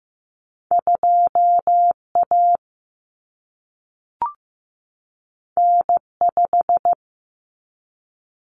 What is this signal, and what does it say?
Morse code, heard as a keyed tone, about 15 wpm, 700 Hz. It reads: 2A N5
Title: Common Cut Numbers - 2-Digit 15wpm